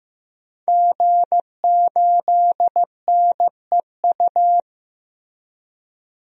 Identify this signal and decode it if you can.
Morse code, heard as a keyed tone, about 15 wpm, 700 Hz. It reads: G8NEU
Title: DXCC-100 Call Signs - 5 Characters - Repeated 2x - 15wpm